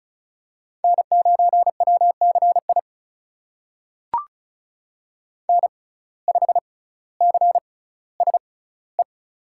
Morse code, heard as a keyed tone, about 36 wpm, spaced as for 18 wpm, 700 Hz. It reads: N9WCI N5CSE
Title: USA Call Signs - 5 Characters - 6x Character Spacing - 35wpm